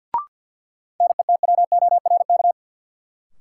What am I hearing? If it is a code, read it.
Morse code, heard as a keyed tone, about 53 wpm, 700 Hz. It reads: NETWORK
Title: Sentences from Top 1000 Words - Review Part 7 - New Word Drill - 50wpm